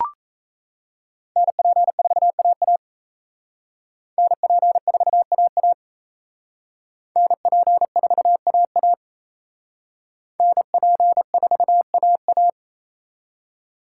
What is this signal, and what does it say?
Morse code, heard as a keyed tone, about 35 wpm, 700 Hz. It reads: NP4AA NP4AA NP4AA NP4AA
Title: Speed Racing - USA 2x2 Call Signs 28wpm